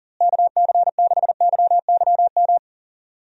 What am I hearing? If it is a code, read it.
Morse code, heard as a keyed tone, about 40 wpm, 700 Hz. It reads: KC6YYM
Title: USA Call Signs - 6 Characters - Repeated 2x - 40wpm